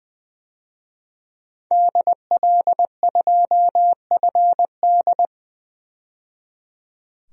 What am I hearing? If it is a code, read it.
Morse code, heard as a keyed tone, about 20 wpm, 700 Hz. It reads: DL2FD
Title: DXCC-100 Call Signs - 5 Characters - 20wpm